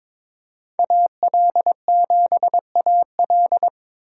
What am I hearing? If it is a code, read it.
Morse code, heard as a keyed tone, about 26 wpm, 700 Hz. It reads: AL7AL
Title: Speed Racing - USA 2x2 Call Signs 22wpm